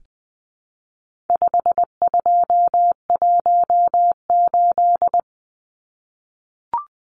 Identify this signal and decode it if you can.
Morse code, heard as a keyed tone, about 20 wpm, 700 Hz. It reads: 5218
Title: Numbers - 4-Digit 20wpm